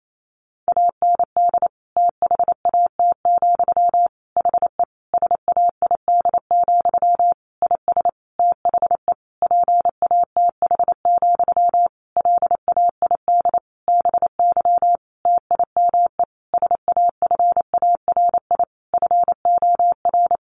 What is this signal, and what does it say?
Morse code, heard as a keyed tone, about 28 wpm, 700 Hz. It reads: AND THAT, HE SAID, IS THE PATH, LAID BY TIME SAFARI FOR
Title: A Sound Of Thunder 28wpm